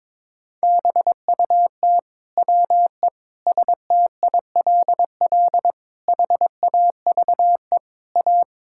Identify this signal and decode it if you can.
Morse code, heard as a keyed tone, about 22 wpm, 700 Hz. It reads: BUT WE STILL HAVE A
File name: Sentences with 9-Word Phrases - 22wpm